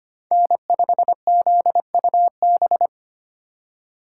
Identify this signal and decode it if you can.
Morse code, heard as a keyed tone, about 25 wpm, 700 Hz. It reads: N5ZUB
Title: USA Call Signs - 4x Character Spacing - 25wpm